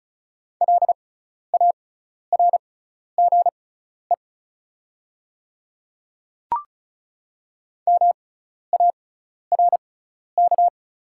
Morse code, heard as a keyed tone, about 36 wpm, spaced as for 12 wpm, 700 Hz. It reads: LARGE MARK
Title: 6x Character Spacing - Top 500 Words - 35wpm